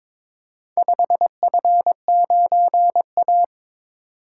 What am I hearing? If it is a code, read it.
Morse code, heard as a keyed tone, about 22 wpm, 700 Hz. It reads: 5F9A
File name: practice / 22wpm / Four Letter-Number Combinations - 22wpm